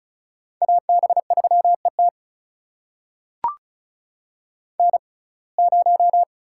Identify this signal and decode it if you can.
Morse code, heard as a keyed tone, about 36 wpm, spaced as for 25 wpm, 700 Hz. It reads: AB3ET N0
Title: USA Call Signs - 5 Characters - 6x Character Spacing - 35wpm